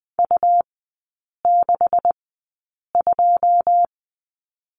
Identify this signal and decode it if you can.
Morse code, heard as a keyed tone, about 20 wpm, spaced as for 8 wpm, 700 Hz. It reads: U62
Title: 127 Lesson 21e - Number 6 - Three Characters 20wpm